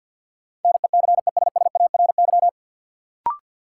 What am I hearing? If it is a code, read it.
Morse code, heard as a keyed tone, about 53 wpm, 700 Hz. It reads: NECESSARY
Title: Sentences from Top 800 Words - New Word Drill - 50wpm